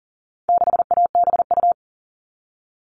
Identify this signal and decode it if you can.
Morse code, heard as a keyed tone, about 40 wpm, 700 Hz. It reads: 6ABU
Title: All Cut Numbers - 4-Digit 40wpm